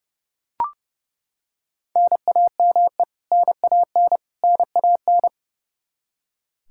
Morse code, heard as a keyed tone, about 30 wpm, 700 Hz. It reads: NAME NAN NAN
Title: K1USN SST QSO Practice 30wpm